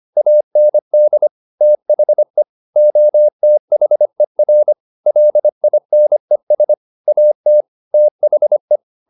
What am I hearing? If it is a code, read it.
Morse code, heard as a keyed tone, about 25 wpm, 600 Hz. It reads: AND THE OTHER LINES AT THE
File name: the old man and the sea part 1 25wpm